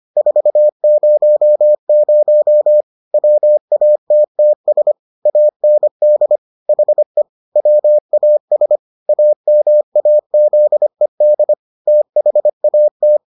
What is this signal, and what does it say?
Morse code, heard as a keyed tone, about 25 wpm, 600 Hz. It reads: 400 WATTS AND HE WAS AMAZED THAT